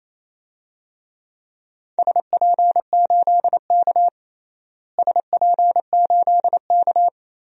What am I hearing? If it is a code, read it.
Morse code, heard as a keyed tone, about 27 wpm, 700 Hz. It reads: SP8K SP8K
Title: DXCC-100 Call Signs - 4 Characters - Repeated 3x - 28wpm